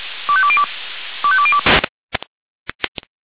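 Five-tone selective call (ZVEI), heard as quick runs of five tones, tones of about 70 ms. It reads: 25202 25202
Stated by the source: ZVEI five-tone selective call received on a software-defined radio.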